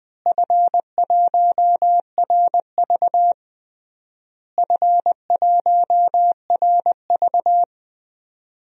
Morse code, heard as a keyed tone, about 21 wpm, 700 Hz. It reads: F1RV F1RV
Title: DXCC-100 Call Signs - 4 Characters - Repeated 3x - 20wpm